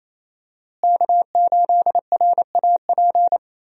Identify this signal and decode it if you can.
Morse code, heard as a keyed tone, about 28 wpm, 700 Hz. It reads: K8RAP
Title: USA Call Signs - 5 Characters - Repeated 3x - 28wpm